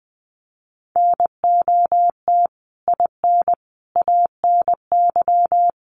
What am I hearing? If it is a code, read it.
Morse code, heard as a keyed tone, about 20 wpm, 700 Hz. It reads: NOT IN ANY
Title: Sentences from Top 1500 Words 20wpm